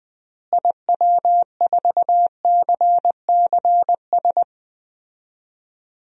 Morse code, heard as a keyed tone, about 21 wpm, 700 Hz. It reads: IW4CCS
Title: DXCC-100 Call Signs - 6 Characters - Repeated 3x - 20wpm